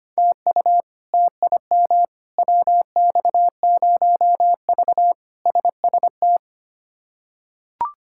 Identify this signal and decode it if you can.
Morse code, heard as a keyed tone, about 25 wpm, 700 Hz. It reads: TU TIM WX0V SST
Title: K1USN SST QSO Practice - 25wpm